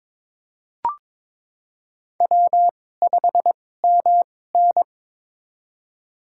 Morse code, heard as a keyed tone, about 21 wpm, spaced as for 16 wpm, 700 Hz. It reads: W5MN
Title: USA Call Signs - 4 Characters - 2x Character Spacing - 22wpm